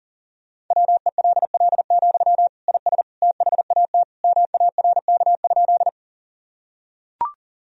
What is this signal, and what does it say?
Morse code, heard as a keyed tone, about 37 wpm, 700 Hz. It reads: WELL, IS THAT MARK?
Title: Sentences from Top 1000 Words - Review Part 1 - 40wpm